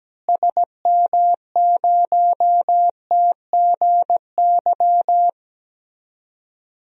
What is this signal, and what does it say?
Morse code, heard as a keyed tone, about 17 wpm, 700 Hz. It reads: SM0TGY
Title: DXCC-100 Call Signs - 6 Characters - Repeated 2x - 17wpm